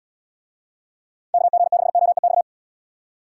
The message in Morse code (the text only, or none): KG6OEX